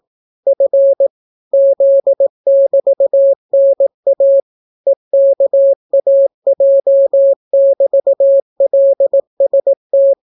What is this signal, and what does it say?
Morse code, heard as a keyed tone, about 18 wpm, 550 Hz. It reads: F Z<BT>NA EKAJ<BT>LST